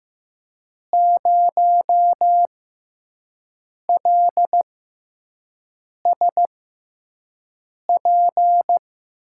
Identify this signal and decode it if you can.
Morse code, heard as a keyed tone, about 15 wpm, spaced as for 5 wpm, 700 Hz. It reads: 0LSP
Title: USA Call Signs - 6 Characters - 6x Character Spacing - 15wpm